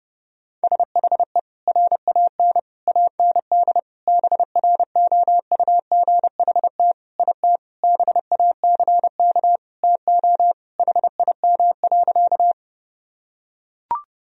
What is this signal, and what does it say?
Morse code, heard as a keyed tone, about 30 wpm, 700 Hz. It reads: SHE RAN AND BROUGHT IT BACK TO HIM.
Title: Sentences from Top 1000 Words - Review Part 3 - 30wpm